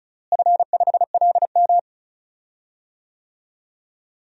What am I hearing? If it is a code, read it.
Morse code, heard as a keyed tone, about 34 wpm, 700 Hz. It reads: F5LM
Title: DXCC-100 Call Signs - 4 Characters - Repeated 3x - 35wpm